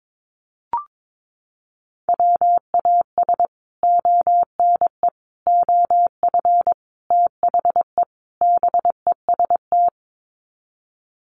Morse code, heard as a keyed tone, about 22 wpm, 700 Hz. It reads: WAS ONE OF THE BEST
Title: Sets of 5 Words Encore2 22wpm